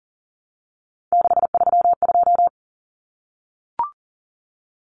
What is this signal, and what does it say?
Morse code, heard as a keyed tone, about 40 wpm, 700 Hz. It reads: 632
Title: Numbers - 3-Digit 40wpm